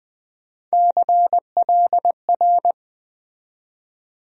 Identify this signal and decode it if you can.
Morse code, heard as a keyed tone, about 21 wpm, 700 Hz. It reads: CLR
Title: USA Call Sign Postfixes - 3 Letters 20wpm